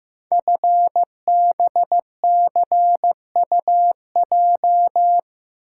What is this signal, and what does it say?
Morse code, heard as a keyed tone, about 15 wpm, 700 Hz. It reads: FBCUJ
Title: Code Groups - 5 Letters - 15wpm